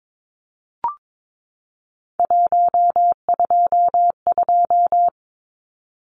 Morse code, heard as a keyed tone, about 22 wpm, 700 Hz. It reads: 122
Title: Numbers - 3-Digit 22wpm